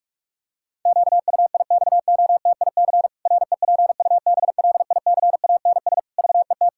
Morse code, heard as a keyed tone, about 46 wpm, 700 Hz. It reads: QUIXOTIC REPUBLICANS VET
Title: Pangram Challenge v2 - 45wpm